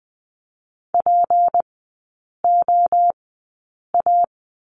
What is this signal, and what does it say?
Morse code, heard as a keyed tone, about 20 wpm, spaced as for 8 wpm, 700 Hz. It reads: POA
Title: 178 Lesson 29e - Slash - Three Characters 20wpm